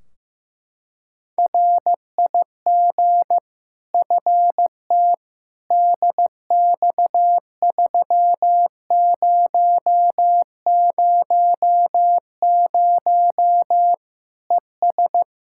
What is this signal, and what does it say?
Morse code, heard as a keyed tone, about 15 wpm, 700 Hz. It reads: RIG FT DX3000 ES
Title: Practice QSO - Exchange 2 - Info - 15wpm